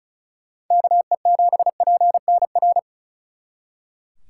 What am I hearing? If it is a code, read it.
Morse code, heard as a keyed tone, about 45 wpm, 700 Hz. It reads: KE7PNR